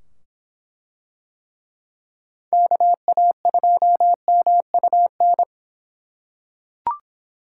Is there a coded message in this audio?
Yes: Morse code from 2.5 to 5.4 s, heard as a keyed tone, about 24 wpm, 700 Hz. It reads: KA2MUN